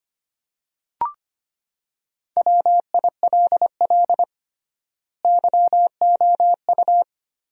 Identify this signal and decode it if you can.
Morse code, heard as a keyed tone, about 25 wpm, 700 Hz. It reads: WILL YOU
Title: 3x Word-Spacing - Sentences from Top 300 Words - 25wpm